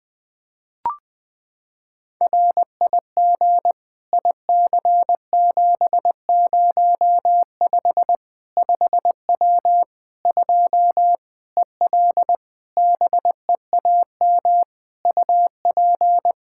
Morse code, heard as a keyed tone, about 20 wpm, 700 Hz. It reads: RIG IC705 5W 2 EL BEAM UP
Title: K1USN SST QSO Practice 20wpm